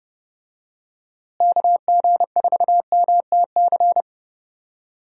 Morse code, heard as a keyed tone, about 31 wpm, 700 Hz. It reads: KG4MTC
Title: USA Call Signs - 6 Characters - Repeated 3x - 30wpm